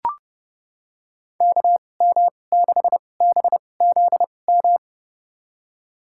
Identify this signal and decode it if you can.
Morse code, heard as a keyed tone, about 30 wpm, spaced as for 22 wpm, 700 Hz. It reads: KM6BZM
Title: USA Call Signs - 6 Characters - 2x Character Spacing - 30wpm